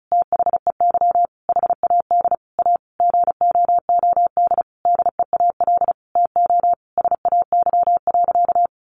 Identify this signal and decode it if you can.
Morse code, heard as a keyed tone, about 34 wpm, 700 Hz. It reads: THEY HAD A GOOD DEAL TO SAY.
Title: Sentences from Top 1000 Words 35wpm